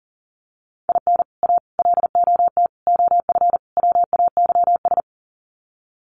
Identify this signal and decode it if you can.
Morse code, heard as a keyed tone, about 40 wpm, 700 Hz. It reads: IN A LOT OF WAYS
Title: Sets of 5 Words Encore 40wpm